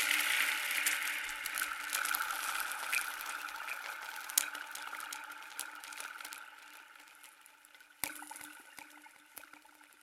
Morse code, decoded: none